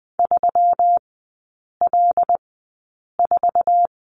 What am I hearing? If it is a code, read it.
Morse code, heard as a keyed tone, about 20 wpm, spaced as for 8 wpm, 690 Hz. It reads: 3L4
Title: 133 Lesson 22e - Question Mark - Three Characters 20wpm